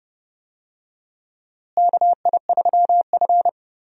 Morse code, heard as a keyed tone, about 29 wpm, 700 Hz. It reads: KI3F
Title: USA Call Signs - 4 Characters - Repeated 2x - 30wpm